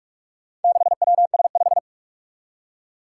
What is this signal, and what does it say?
Morse code, heard as a keyed tone, about 44 wpm, 700 Hz. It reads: 6WS5
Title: Four Challenging Letter-Number Combinations 45wpm